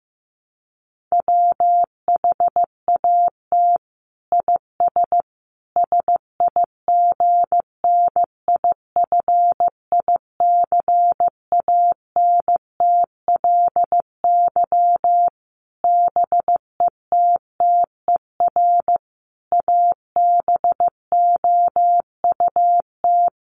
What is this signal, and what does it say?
Morse code, heard as a keyed tone, about 15 wpm, 700 Hz. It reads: WHAT IS SIGNIFICANTLY BETTER ABOUT
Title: Sentences from Top 1500 Words 15wpm